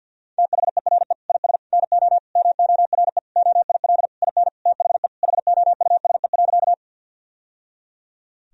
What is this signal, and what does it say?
Morse code, heard as a keyed tone, about 50 wpm, 700 Hz. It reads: THERE IS NO MORE OIL IN THE HOUSE.
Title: Sentences from Top 1000 Words - Review Part 4 - 50wpm